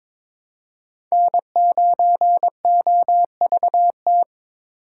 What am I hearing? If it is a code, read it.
Morse code, heard as a keyed tone, about 22 wpm, 700 Hz. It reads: N9OVT
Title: USA Call Signs - 5 Characters - 4x Character Spacing - 22wpm